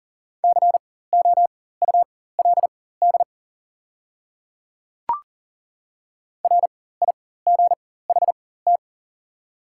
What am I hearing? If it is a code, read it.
Morse code, heard as a keyed tone, about 40 wpm, spaced as for 19 wpm, 700 Hz. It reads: COULD RIGHT
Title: 4x Character Spacing - Top 200 Words - 40wpm